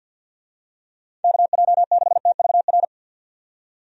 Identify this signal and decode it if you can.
Morse code, heard as a keyed tone, about 45 wpm, 700 Hz. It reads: KJ6TVR
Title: USA Call Signs - 6 Characters - 50wpm